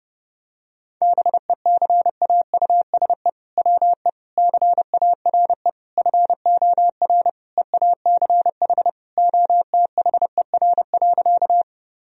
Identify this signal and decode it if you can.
Morse code, heard as a keyed tone, about 30 wpm, 700 Hz. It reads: BECAUSE WE CARE FOR EACH OTHER.
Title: Sentences from Top 1000 Words - Review Part 3 - 30wpm